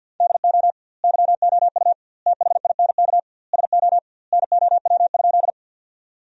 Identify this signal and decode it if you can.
Morse code, heard as a keyed tone, about 46 wpm, 700 Hz. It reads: DO YOU THINK SO NOW?
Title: 2x Word-Spacing - Sentences from Top 200 Words - 50wpm